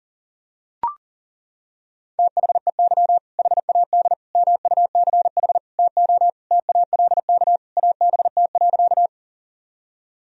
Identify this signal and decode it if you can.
Morse code, heard as a keyed tone, about 40 wpm, 700 Hz. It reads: THEY HAD MUCH TO TALK ABT.
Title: Sentences from Top 400 Words and Abbreviations 40wpm